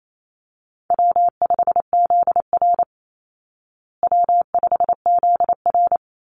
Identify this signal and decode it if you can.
Morse code, heard as a keyed tone, about 28 wpm, 700 Hz. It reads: W5ZR W5ZR
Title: Easy Call Signs Repeated 2x - 28wpm